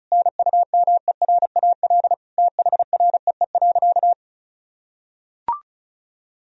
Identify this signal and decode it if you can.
Morse code, heard as a keyed tone, about 34 wpm, 700 Hz. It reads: NUMERAL THREE.
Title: Sentences from Top 1000 Words - Review Part 2 - 35wpm